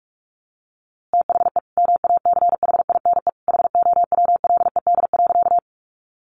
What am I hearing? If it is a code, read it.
Morse code, heard as a keyed tone, about 44 wpm, 700 Hz. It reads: THE MACHINE HOWLED.
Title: A Sound Of Thunder 45wpm